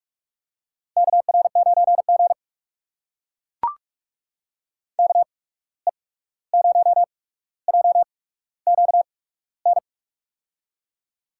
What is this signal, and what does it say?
Morse code, heard as a keyed tone, about 44 wpm, spaced as for 17 wpm, 700 Hz. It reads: KR9G KE0JQN
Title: USA Call Signs - 8x Character Spacing - 45wpm